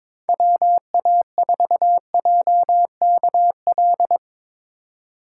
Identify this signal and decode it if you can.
Morse code, heard as a keyed tone, about 22 wpm, 700 Hz. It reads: WA4JKL
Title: K1USN SST QSO Practice - 22wpm